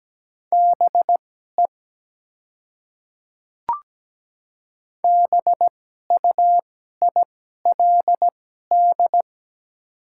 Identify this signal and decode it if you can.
Morse code, heard as a keyed tone, about 17 wpm, spaced as for 12 wpm, 700 Hz. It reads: BE BUILD
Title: 2x Character Spacing - Top 200 Words - 17wpm